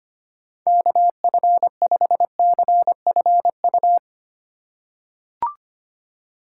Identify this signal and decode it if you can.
Morse code, heard as a keyed tone, about 25 wpm, 700 Hz. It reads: KF5CFU
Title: USA Call Signs - 6 Characters - 2x Character Spacing - 25wpm